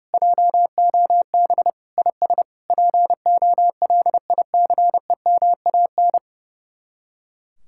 Morse code, heard as a keyed tone, about 37 wpm, 700 Hz. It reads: JOB IS POLICEMAN
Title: Speed-Racing Common QSO Phrases - 30wpm